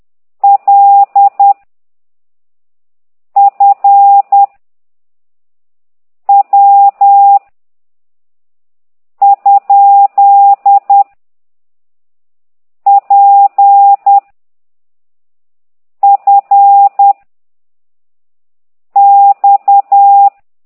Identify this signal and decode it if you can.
Morse code, heard as a keyed tone, about 10 wpm, 800 Hz. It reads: L F W ? P F X
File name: ct2idw / exerc9